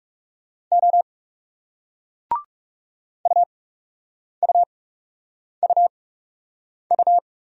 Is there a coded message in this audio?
Yes: Morse code from 0.7 to 7.2 s, heard as a keyed tone, about 38 wpm, spaced as for 8 wpm, 700 Hz. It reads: O UUUU